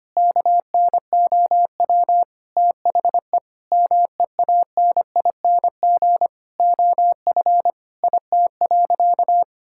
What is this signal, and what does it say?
Morse code, heard as a keyed tone, about 25 wpm, 700 Hz. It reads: KNOW THE MEANING OF IT.
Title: Sentences from Top 1000 Words - Review Part 6 - 25wpm